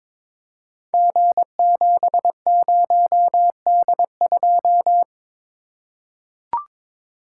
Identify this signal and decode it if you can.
Morse code, heard as a keyed tone, about 22 wpm, 700 Hz. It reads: G70D2
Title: Code Groups - 5 Letter:Numbers - 22wpm